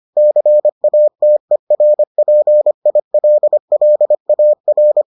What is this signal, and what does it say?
Morse code, heard as a keyed tone, about 25 wpm, 600 Hz. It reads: CATERPILLAR